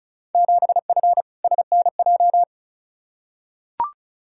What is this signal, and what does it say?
Morse code, heard as a keyed tone, about 36 wpm, 700 Hz. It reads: 7F SNJ
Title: Field Day Exchanges Encore - 35wpm